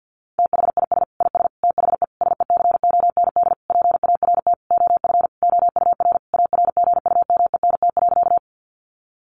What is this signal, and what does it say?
Morse code, heard as a keyed tone, about 50 wpm, 700 Hz. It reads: THIS IS THE SECOND PART OF OUR ARGUMENT.
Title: Sentences from Top 1400 Words 50wpm